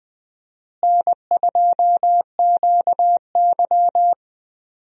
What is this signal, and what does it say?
Morse code, heard as a keyed tone, about 21 wpm, 700 Hz. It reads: N2QY
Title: USA Call Signs - 4 Characters - Repeated 2x - 20wpm